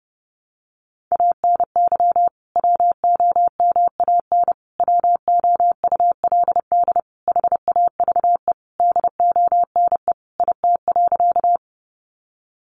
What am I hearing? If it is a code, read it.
Morse code, heard as a keyed tone, about 30 wpm, 700 Hz. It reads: ANY WOMAN WOULD HAVE DONE IT.
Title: Sentences from Top 1100 Words 30wpm